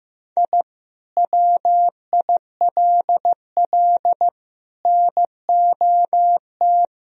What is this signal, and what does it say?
Morse code, heard as a keyed tone, about 15 wpm, 700 Hz. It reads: I WILL NOT
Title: Sentences from Top 1000 Words - Review Part 1 - 15wpm